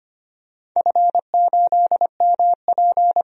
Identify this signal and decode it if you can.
Morse code, heard as a keyed tone, about 25 wpm, 700 Hz. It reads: F8MP
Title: DXCC-100 Call Signs - 4 Characters - 25wpm